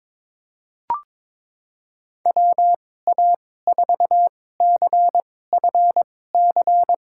Morse code, heard as a keyed tone, about 22 wpm, spaced as for 16 wpm, 700 Hz. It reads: WA4CFC
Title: USA Call Signs - 6 Characters - 2x Character Spacing - 22wpm